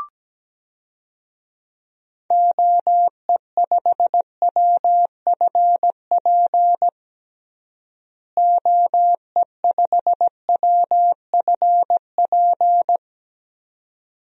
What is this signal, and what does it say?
Morse code, heard as a keyed tone, about 17 wpm, 700 Hz. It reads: OE5WFP OE5WFP
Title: DXCC-100 Call Signs - 6 Characters - Repeated 2x - 17wpm